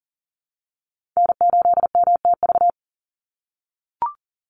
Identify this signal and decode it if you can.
Morse code, heard as a keyed tone, about 40 wpm, 700 Hz. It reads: N8MTV
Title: Call Signs - 40wpm